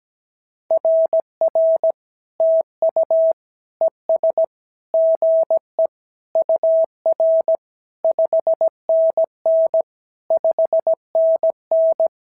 Morse code, heard as a keyed tone, about 17 wpm, 650 Hz. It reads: RR TU ES GE UR 5NN 5NN
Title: SOTA QSO Practice 17wpm